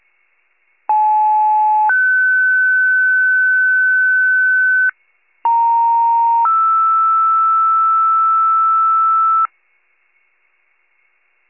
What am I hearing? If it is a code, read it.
Two-tone page, heard as a one-second tone, then a three-second tone, about 860 then 1500 Hz, about 930 then 1320 Hz. It is heard from 0.9 to 9.5 s.